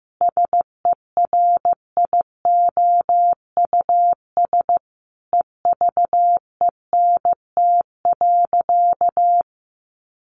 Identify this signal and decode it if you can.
Morse code, heard as a keyed tone, about 15 wpm, 700 Hz. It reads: SERIOUS EVENT.
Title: Sentences from Top 1400 Words 15wpm